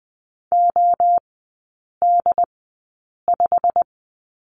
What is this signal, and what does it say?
Morse code, heard as a keyed tone, about 20 wpm, spaced as for 8 wpm, 700 Hz. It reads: OD5
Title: 087 Lesson 15e - Number 5 - Three Characters 20wpm